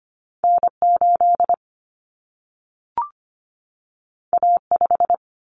Morse code, heard as a keyed tone, about 25 wpm, 700 Hz. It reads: N8 A5
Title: Common Cut Numbers - 2-Digit 25wpm